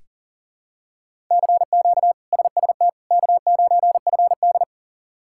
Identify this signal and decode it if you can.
Morse code, heard as a keyed tone, about 40 wpm, 700 Hz. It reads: CQ SST K9FD
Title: K1USN SST QSO Practice - 40wpm